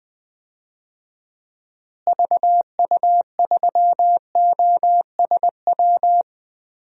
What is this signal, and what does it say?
Morse code, heard as a keyed tone, about 20 wpm, 700 Hz. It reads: VU3OSW
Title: DXCC-100 Call Signs - 6 Characters - Repeated 2x - 20wpm